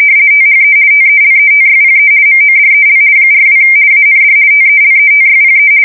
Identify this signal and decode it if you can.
RTTY, about 50 baud. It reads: THE TELETYPE MODEL 42 AND TELETYPE
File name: fsk170-50B / teletype-corp